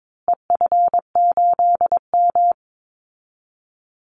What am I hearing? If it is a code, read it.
Morse code, heard as a keyed tone, about 22 wpm, 700 Hz. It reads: EF8M
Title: Call Signs Repeated 2x - 22wpm